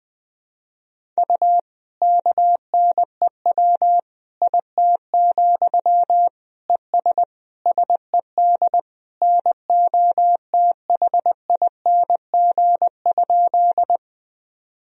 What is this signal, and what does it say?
Morse code, heard as a keyed tone, about 20 wpm, 700 Hz. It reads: U KNEW IT, ES SED NOTHING?
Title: Sentences from Top 500 Words and Abbreviations 20wpm